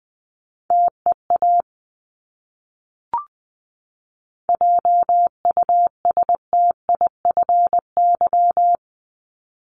Morse code, heard as a keyed tone, about 19 wpm, 700 Hz. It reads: TEA JUSTIFY
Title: Top 1751-2000 Words 20wpm